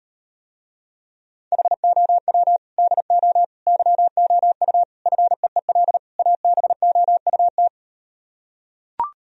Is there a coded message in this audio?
Yes: Morse code from 1.5 to 7.7 s, heard as a keyed tone, about 38 wpm, 700 Hz. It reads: HOW DO YOU FEEL ABOUT